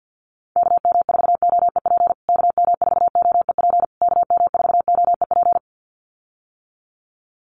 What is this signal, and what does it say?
Morse code, heard as a keyed tone, about 50 wpm, 700 Hz. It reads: KM4OEP KM4OEP KM4OEP
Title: USA Call Signs Repeated 3x 50wpm